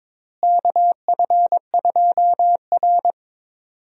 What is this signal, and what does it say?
Morse code, heard as a keyed tone, about 21 wpm, 700 Hz. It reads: KF2R
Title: USA Call Signs - 4 Characters - Repeated 2x - 22wpm